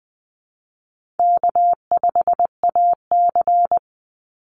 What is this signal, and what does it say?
Morse code, heard as a keyed tone, about 20 wpm, 700 Hz. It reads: K5AC
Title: USA Call Signs 20wpm